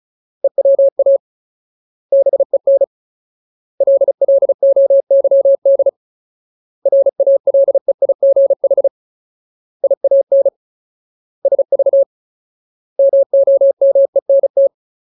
Morse code, heard as a keyed tone, about 36 wpm, 550 Hz. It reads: EWA BEN LLOYD RALEIGH IAN SV MOMENT
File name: Ultimate Word List / 35WPM Ultimate